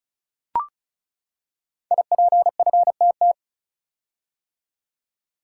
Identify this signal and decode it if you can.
Morse code, heard as a keyed tone, about 36 wpm, 690 Hz. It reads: IPFTT
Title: Code Groups - 5 Letters - 35wpm